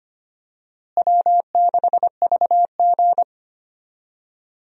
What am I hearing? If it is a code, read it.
Morse code, heard as a keyed tone, about 25 wpm, 700 Hz. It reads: W6VG
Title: USA Call Signs - 4 Characters - Repeated 2x - 25wpm